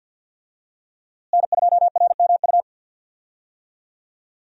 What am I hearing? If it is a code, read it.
Morse code, heard as a keyed tone, about 50 wpm, 700 Hz. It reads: N2RMU
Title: USA Call Signs - 5 Characters - Repeated 2x - 50wpm